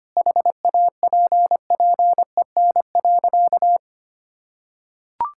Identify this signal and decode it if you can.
Morse code, heard as a keyed tone, about 25 wpm, 700 Hz. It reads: HAPPEN.
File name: practice / 25wpm / Sentences with 9-Word Phrases - 25wpm